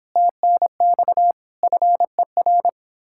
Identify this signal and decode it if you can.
Morse code, heard as a keyed tone, about 29 wpm, 700 Hz. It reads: TNX FER